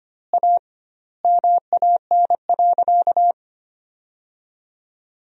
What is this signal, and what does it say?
Morse code, heard as a keyed tone, about 25 wpm, 700 Hz. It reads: A MAN.
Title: 2x Word-Spacing - Sentences from Top 300 Words - 25wpm